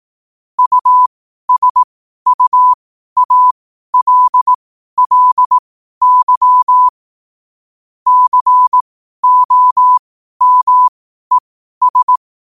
Morse code, heard as a keyed tone, about 18 wpm, spaced as for 12 wpm, 1000 Hz. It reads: USUALLY COMES
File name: qotdfiles / 12wpm